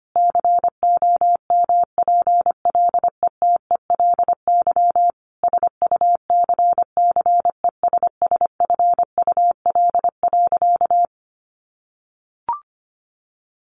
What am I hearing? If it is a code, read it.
Morse code, heard as a keyed tone, about 25 wpm, 700 Hz. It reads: COMPLETELY SUCCESSFUL.
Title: Sentences from Top 1100 Words 25wpm